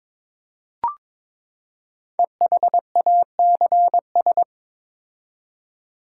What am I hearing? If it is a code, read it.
Morse code, heard as a keyed tone, about 22 wpm, 700 Hz. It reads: EHACS
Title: Code Groups - 5 Letters - 22wpm